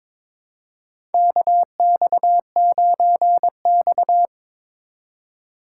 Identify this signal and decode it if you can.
Morse code, heard as a keyed tone, about 22 wpm, 700 Hz. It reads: KX9X
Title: USA Call Signs - 4 Characters - 4x Character Spacing - 22wpm